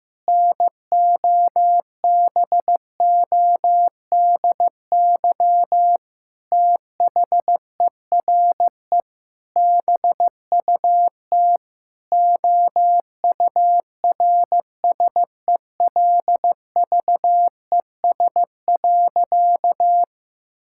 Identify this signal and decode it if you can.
Morse code, heard as a keyed tone, about 14 wpm, 700 Hz. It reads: NOBODY THERE BUT OURSELVES.
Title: Sentences from Top 1600 Words 15wpm